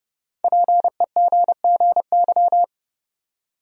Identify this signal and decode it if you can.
Morse code, heard as a keyed tone, about 32 wpm, 700 Hz. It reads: PEGGY
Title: USA Common Names - Max 5 Letters - 30wpm